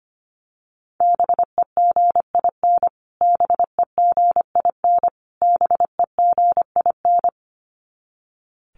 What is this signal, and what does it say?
Morse code, heard as a keyed tone, about 25 wpm, 700 Hz. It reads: BEGIN BEGIN BEGIN
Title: Top 500 Words Repeated 25wpm